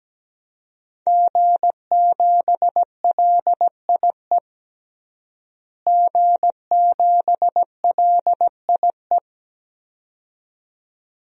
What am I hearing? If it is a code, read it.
Morse code, heard as a keyed tone, about 17 wpm, 700 Hz. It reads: G7LIE G7LIE
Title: DXCC-100 Call Signs - 5 Characters - Repeated 2x - 17wpm